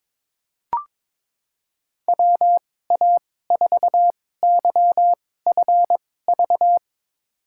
Morse code, heard as a keyed tone, about 22 wpm, spaced as for 16 wpm, 700 Hz. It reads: WA4YFV
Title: USA Call Signs - 2x Character Spacing - 22wpm